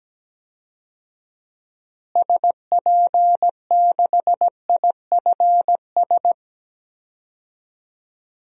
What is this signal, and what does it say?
Morse code, heard as a keyed tone, about 17 wpm, 700 Hz. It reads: SP6IFS
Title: DXCC-100 Call Signs - 6 Characters - Repeated 2x - 17wpm